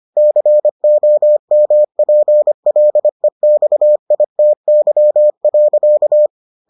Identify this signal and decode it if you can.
Morse code, heard as a keyed tone, about 25 wpm, 600 Hz. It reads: COMPLEXITY.